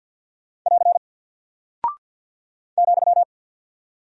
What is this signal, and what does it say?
Morse code, heard as a keyed tone, about 48 wpm, 700 Hz. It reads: <AR> ,
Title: Single Character (Letter:Number:Punctuation:Prosign) Alt 50wpm